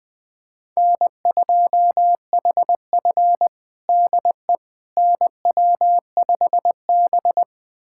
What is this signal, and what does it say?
Morse code, heard as a keyed tone, about 20 wpm, 700 Hz. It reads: N2HF DE NW5B
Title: USA Callsign DE Callsign - 4 Characters - 20wpm